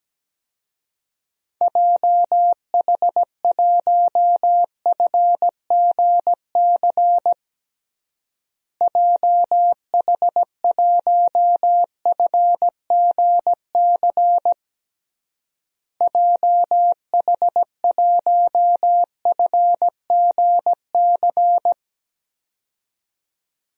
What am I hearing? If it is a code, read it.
Morse code, heard as a keyed tone, about 17 wpm, 700 Hz. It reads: JH1FGC JH1FGC JH1FGC
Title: DXCC-100 Call Signs - 6 Characters - Repeated 3x - 17wpm